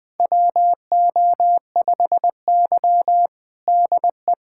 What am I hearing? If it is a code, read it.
Morse code, heard as a keyed tone, about 20 wpm, 700 Hz. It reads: WO5Y DE
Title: USA Callsign DE Callsign - 4 Characters - 20wpm